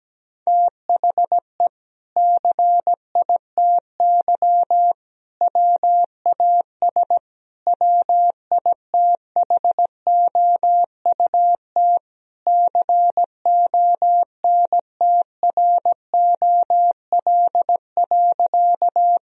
Morse code, heard as a keyed tone, about 17 wpm, 700 Hz. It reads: THE CITY WAS WITHOUT CONTROL.
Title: Sentences from Top 1000 Words - Review Part 3 - 17wpm